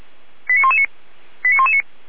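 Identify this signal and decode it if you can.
Five-tone selective call (ZVEI), heard as quick runs of five tones, tones of about 80 ms. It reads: 89119 89119
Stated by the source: ZVEI five-tone selective call received on an SDR.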